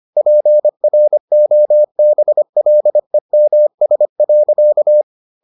Morse code, heard as a keyed tone, about 25 wpm, 600 Hz. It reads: PROBLEMS.